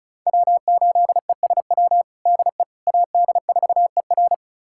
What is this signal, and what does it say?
Morse code, heard as a keyed tone, about 34 wpm, 700 Hz. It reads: W8ESW DE AD4ER
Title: USA Callsign DE Callsign - 5 Characters - 35wpm